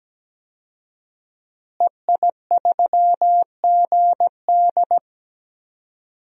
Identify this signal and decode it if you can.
Morse code, heard as a keyed tone, about 17 wpm, 700 Hz. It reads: EI3GD
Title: DXCC-100 Call Signs - 5 Characters - Repeated 2x - 17wpm